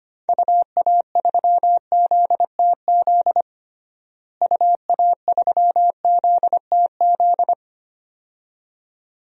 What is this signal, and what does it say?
Morse code, heard as a keyed tone, about 25 wpm, 700 Hz. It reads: UA3ZTZ UA3ZTZ
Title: DXCC-100 Call Signs - 6 Characters - Repeated 2x - 25wpm